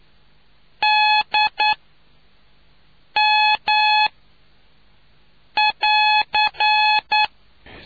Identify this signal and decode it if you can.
Morse code, heard as a keyed tone, about 9 wpm, spaced as for 5 wpm, 850 Hz. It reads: DM<AR>